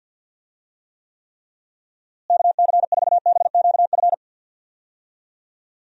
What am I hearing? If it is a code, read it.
Morse code, heard as a keyed tone, about 46 wpm, 700 Hz. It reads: KC4BQF
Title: USA Call Signs - 6 Characters - Repeated 3x - 50wpm